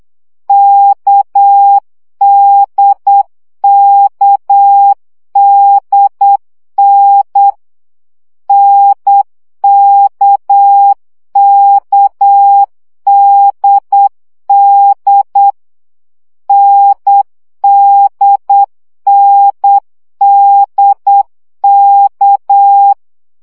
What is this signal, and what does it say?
Morse code, heard as a keyed tone, about 8 wpm, 800 Hz. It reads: KDKDN NKKDD NDNDK
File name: CW gyakorlas n d k 7 baud